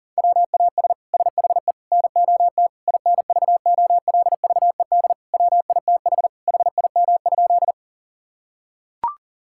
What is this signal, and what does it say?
Morse code, heard as a keyed tone, about 40 wpm, 700 Hz. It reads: WAS SHE NOT INVOLVED WITH HIM?
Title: Sentences from Top 1000 Words - Review Part 5 - 40wpm